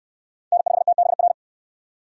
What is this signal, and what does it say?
Morse code, heard as a keyed tone, about 59 wpm, 700 Hz. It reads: N5TBR